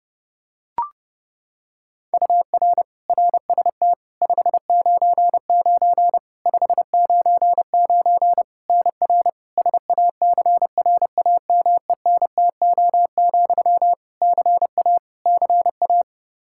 Morse code, heard as a keyed tone, about 30 wpm, 700 Hz. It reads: UR RST 599 599 NR SACRAMENTO, CA CA
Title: K1USN SST QSO Practice 30wpm